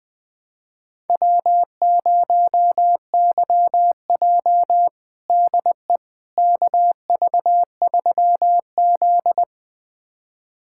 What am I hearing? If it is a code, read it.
Morse code, heard as a keyed tone, about 20 wpm, 700 Hz. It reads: W0YJ DE KV3Z
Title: USA Callsign DE Callsign - 4 Characters - 20wpm